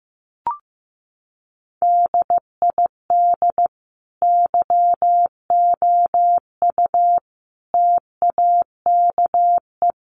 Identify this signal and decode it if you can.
Morse code, heard as a keyed tone, about 15 wpm, 700 Hz. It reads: DID YOU TAKE
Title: Sentences from Top 1200 Words 15wpm